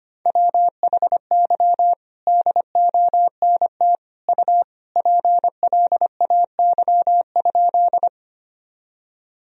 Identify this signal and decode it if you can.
Morse code, heard as a keyed tone, about 25 wpm, 700 Hz. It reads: WHY DONT U PLAY?
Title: Sentences from Top 300 Words and Abbreviations 25wpm